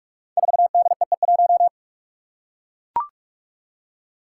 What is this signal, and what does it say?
Morse code, heard as a keyed tone, about 44 wpm, 700 Hz. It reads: 4DEE1
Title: Five Challenging Letter-Number Combinations 45wpm